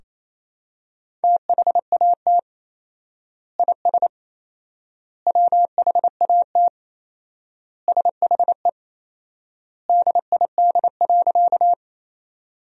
Morse code, heard as a keyed tone, about 28 wpm, 700 Hz. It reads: THAT IS WHAT SHE DID.
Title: 4x Word-Spacing - Sentences from Top 100 Words - 28wpm